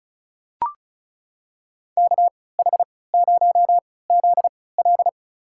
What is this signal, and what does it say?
Morse code, heard as a keyed tone, about 36 wpm, spaced as for 20 wpm, 700 Hz. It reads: KH0ZL
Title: USA Call Signs - 5 Characters - 3x Character Spacing - 35wpm